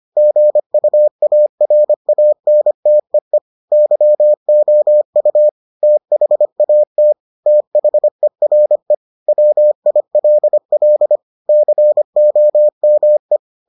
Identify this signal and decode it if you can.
Morse code, heard as a keyed tone, about 25 wpm, 600 Hz. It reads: GUARANTEE YOU THAT THERE WILL COME